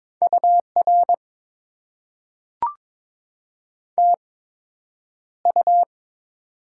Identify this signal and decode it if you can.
Morse code, heard as a keyed tone, about 22 wpm, spaced as for 9 wpm, 700 Hz. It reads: UR TU
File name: Top 90 QSO Elements - 8x Character Spacing - 22wpm